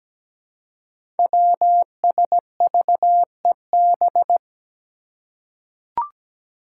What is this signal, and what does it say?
Morse code, heard as a keyed tone, about 17 wpm, 700 Hz. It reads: WSVEB
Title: Five Challenging Letter-Number Combinations 17wpm